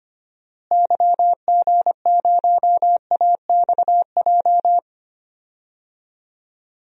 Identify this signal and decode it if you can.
Morse code, heard as a keyed tone, about 25 wpm, 690 Hz. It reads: YG0AXJ
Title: DXCC-100 Call Signs - 6 Characters - Repeated 3x - 25wpm